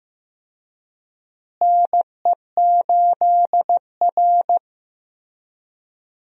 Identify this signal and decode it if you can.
Morse code, heard as a keyed tone, about 15 wpm, 700 Hz. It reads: NE8R
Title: USA Call Signs - 4 Characters - Repeated 3x - 15wpm